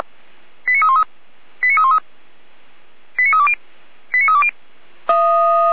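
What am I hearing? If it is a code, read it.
Five-tone selective call (ZVEI), heard as quick runs of five tones, tones of about 70 ms. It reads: 89313 89313 89320 89320
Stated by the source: ZVEI five-tone selective call received on an SDR, from a feuerwehr-orlamuende.org fire brigade audio archive.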